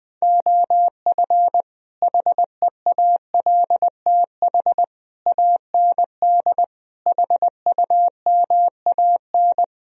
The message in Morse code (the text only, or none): OF HEALTH AND HUMAN